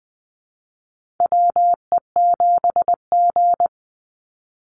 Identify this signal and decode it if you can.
Morse code, heard as a keyed tone, about 20 wpm, 700 Hz. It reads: WE7G
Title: Call Signs Repeated 3x - 20wpm